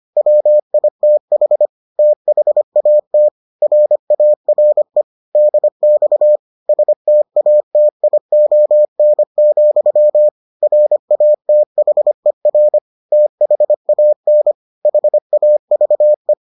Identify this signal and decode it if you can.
Morse code, heard as a keyed tone, about 25 wpm, 600 Hz. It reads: WITH THAT RARE DX STATION, RATHER THAN HAVE